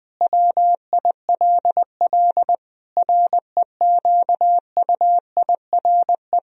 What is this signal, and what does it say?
Morse code, heard as a keyed tone, about 20 wpm, 700 Hz. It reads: WILL REQUIRE
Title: Sentences from Top 1000 Words - Review Part 4 - 20wpm